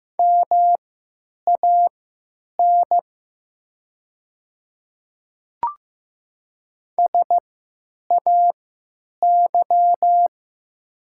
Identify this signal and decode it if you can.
Morse code, heard as a keyed tone, about 16 wpm, spaced as for 9 wpm, 700 Hz. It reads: MAN SAY
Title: 3x Character Spacing - Top 200 Words - 15wpm